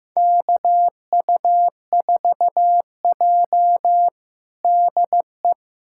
Morse code, heard as a keyed tone, about 15 wpm, 700 Hz. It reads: KU4J DE
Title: USA Callsign DE Callsign - 4 Characters - 15wpm